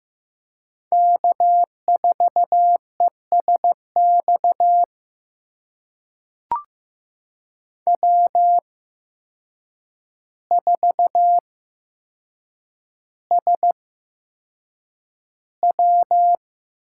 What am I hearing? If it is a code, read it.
Morse code, heard as a keyed tone, about 15 wpm, spaced as for 7 wpm, 700 Hz. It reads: K4ESX W4SW
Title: USA Call Signs - 5 Characters - 8x Character Spacing - 15wpm